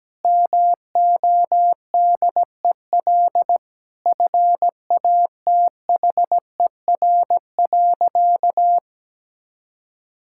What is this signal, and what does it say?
Morse code, heard as a keyed tone, about 17 wpm, 700 Hz. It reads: MODEL FATHER.
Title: Sentences from Top 1000 Words - Review Part 5 - 17wpm